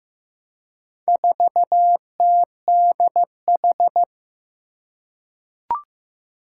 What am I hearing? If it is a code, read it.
Morse code, heard as a keyed tone, about 15 wpm, 700 Hz. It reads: 4TDH
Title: Four Challenging Letter-Number Combinations 15wpm